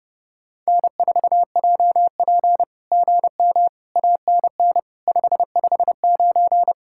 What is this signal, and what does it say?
Morse code, heard as a keyed tone, about 26 wpm, 700 Hz. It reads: N4JP GM ANN 559
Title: SOTA QSO Practice 30wpm